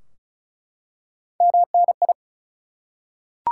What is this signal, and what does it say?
Morse code, heard as a keyed tone, about 36 wpm, 700 Hz. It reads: MNI
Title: Top 180 QSO Elements 35wpm